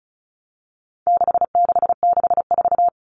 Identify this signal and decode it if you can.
Morse code, heard as a keyed tone, about 34 wpm, 700 Hz. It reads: 6664
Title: Numbers - 4-Digit 35wpm